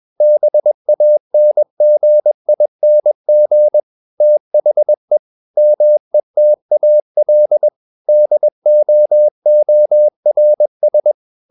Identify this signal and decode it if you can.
Morse code, heard as a keyed tone, about 21 wpm, 600 Hz. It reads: BANGING THE METAL DOORS
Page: CW med Gnister